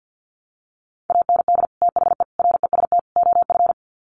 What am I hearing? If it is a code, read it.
Morse code, heard as a keyed tone, about 50 wpm, 700 Hz. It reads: AND THE REST OF
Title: Sets of 4 Words 50wpm